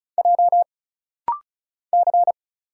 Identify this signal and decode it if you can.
Morse code, heard as a keyed tone, about 36 wpm, 700 Hz. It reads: J C
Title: Single Letters - Rapid-Fire 35wpm